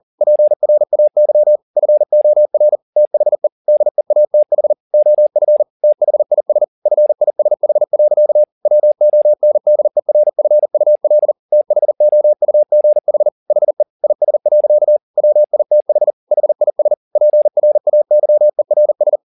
Morse code, heard as a keyed tone, about 40 wpm, 600 Hz. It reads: PRAY FOR THE DEATH OF THIS FISH. WONDERFUL THOUGH HE IS. WITH HIS PRAYERS